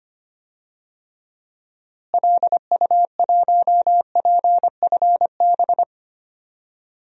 Morse code, heard as a keyed tone, about 25 wpm, 700 Hz. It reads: LU1PFB
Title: DXCC-100 Call Signs - 6 Characters - Repeated 2x - 25wpm